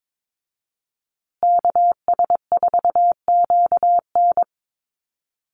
Morse code, heard as a keyed tone, about 22 wpm, 700 Hz. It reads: KS4QN
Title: USA Call Signs Repeated 3x 22wpm